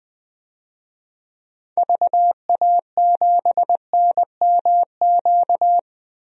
Morse code, heard as a keyed tone, about 20 wpm, 700 Hz. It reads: VA7NMQ
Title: DXCC-100 Call Signs - 6 Characters - 20wpm